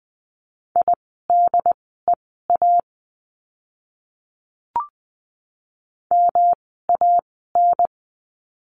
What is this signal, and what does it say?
Morse code, heard as a keyed tone, about 20 wpm, spaced as for 15 wpm, 700 Hz. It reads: IDEA MAN
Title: 108 Lesson 18f - Letter m - Words 20wpm